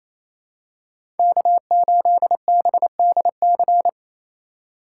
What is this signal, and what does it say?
Morse code, heard as a keyed tone, about 28 wpm, 700 Hz. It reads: K8BDC
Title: USA Call Signs - 8x Character Spacing - 28wpm